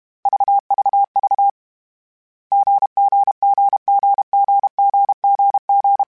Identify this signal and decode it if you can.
Morse code, heard as a keyed tone, about 32 wpm, 800 Hz. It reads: VVV GGGGGGGG
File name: Short Runs / Short Runs-mp3s / GZ GZ7 @30